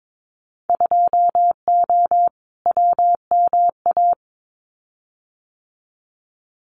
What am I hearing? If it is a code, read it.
Morse code, heard as a keyed tone, about 22 wpm, 700 Hz. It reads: 2O WMA
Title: Winter Field Day Exchanges Encore 22wpm